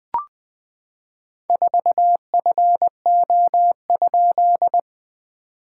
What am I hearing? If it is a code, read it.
Morse code, heard as a keyed tone, about 20 wpm, 660 Hz. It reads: 4FO?
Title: SOTA QSO Practice 20wpm